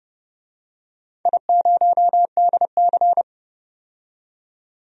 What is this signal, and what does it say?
Morse code, heard as a keyed tone, about 32 wpm, 700 Hz. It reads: I0DC
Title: DXCC-100 Call Signs - 4 Characters - 30wpm